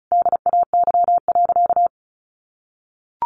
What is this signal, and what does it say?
Morse code, heard as a keyed tone, about 34 wpm, 700 Hz. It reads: DAY.
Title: Sentences from Top 1200 Words 35wpm